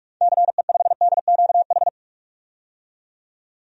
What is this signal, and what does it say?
Morse code, heard as a keyed tone, about 46 wpm, 700 Hz. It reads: CE5DQH
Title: DXCC-100 Call Signs - 6 Characters - 45wpm